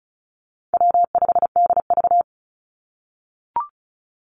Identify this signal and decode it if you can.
Morse code, heard as a keyed tone, about 36 wpm, 700 Hz. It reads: W5DV
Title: Call Signs - 35wpm